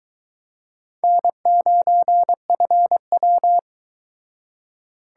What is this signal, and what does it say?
Morse code, heard as a keyed tone, about 23 wpm, 700 Hz. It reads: N9FW